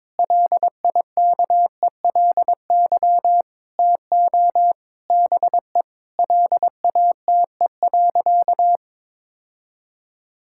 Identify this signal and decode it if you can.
Morse code, heard as a keyed tone, about 22 wpm, 700 Hz. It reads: LIKELY TO BE LATE.
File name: practice / 22wpm / Sentences with 4-Word Phrases - 22wpm